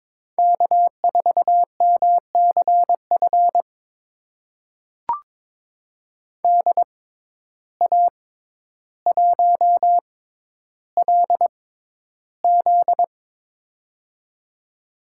Morse code, heard as a keyed tone, about 22 wpm, spaced as for 11 wpm, 700 Hz. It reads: K4MCF DA1LZ
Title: USA Call Signs - 5 Characters - 6x Character Spacing - 22wpm